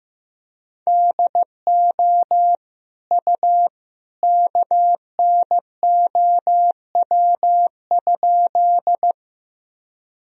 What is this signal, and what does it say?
Morse code, heard as a keyed tone, about 15 wpm, 700 Hz. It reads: DO U KNOW?
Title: Sentences from Top 100 Words and Abbreviations 15wpm